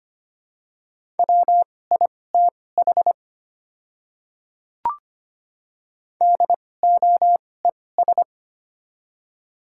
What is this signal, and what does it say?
Morse code, heard as a keyed tone, about 25 wpm, spaced as for 18 wpm, 700 Hz. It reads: WITH DOES
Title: 2x Character Spacing - Top 200 Words - 25wpm